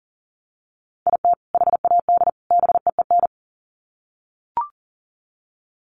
Morse code, heard as a keyed tone, about 40 wpm, 700 Hz. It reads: IT HAD BEEN
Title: Sets of 3 Words 40wpm